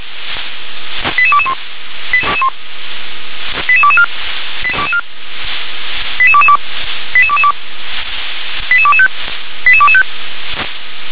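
Five-tone selective call (ZVEI), heard as quick runs of five tones, tones of about 70 ms. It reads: 88221 88224 88222 88225 88225
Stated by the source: ZVEI five-tone selective call received on an SDR.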